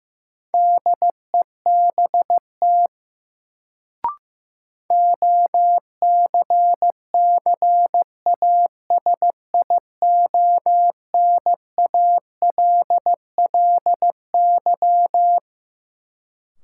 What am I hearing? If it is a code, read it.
Morse code, heard as a keyed tone, about 15 wpm, 700 Hz. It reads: DEBT OCCASIONALLY
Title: Sentences from Top 1300 Words - New Word Drill - 15wpm